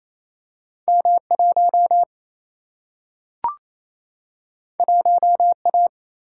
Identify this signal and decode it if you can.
Morse code, heard as a keyed tone, about 28 wpm, 700 Hz. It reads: M 1 1 A
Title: Single-Letter Single-Number 28wpm